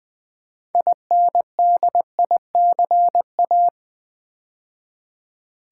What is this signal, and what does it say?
Morse code, heard as a keyed tone, about 20 wpm, 700 Hz. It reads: INDICA
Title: Build A Word - 3 Letters At A Time - 8-Letter Words - 20wpm